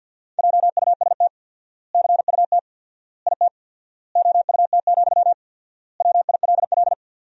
Morse code, heard as a keyed tone, about 47 wpm, 700 Hz. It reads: JUST CUT IT OUT, WILL
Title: 4x Word-Spacing - Sentences from Top 300 Words - 50wpm